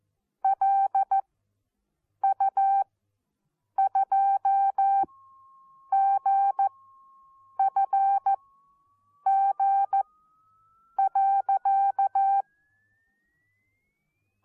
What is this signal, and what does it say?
Morse code, heard as a keyed tone, about 14 wpm, spaced as for 7 wpm, 780 Hz. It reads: LU2GFG.
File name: Onlinelehrgang / morsen / DJ4UF Morsen14